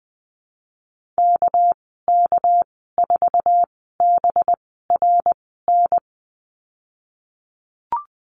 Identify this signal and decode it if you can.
Morse code, heard as a keyed tone, about 19 wpm, spaced as for 14 wpm, 700 Hz. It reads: KK4BRN
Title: 213 Lesson 34g - Number 8 - Call Signs 20wpm